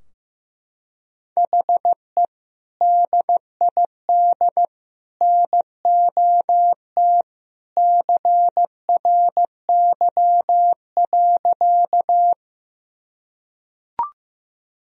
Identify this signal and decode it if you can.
Morse code, heard as a keyed tone, about 15 wpm, 700 Hz. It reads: HE DID NOT CRY.
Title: Sentences from Top 1000 Words - Review Part 2 - 15wpm